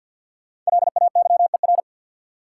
Morse code, heard as a keyed tone, about 49 wpm, 700 Hz. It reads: LAYER
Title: Sentences from Top 1600 Words - New Word Drill Alt - 50wpm